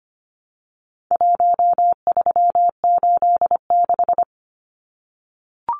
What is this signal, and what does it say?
Morse code, heard as a keyed tone, about 25 wpm, 700 Hz. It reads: 1386
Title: Numbers - Contest Style - 4-Digit 25wpm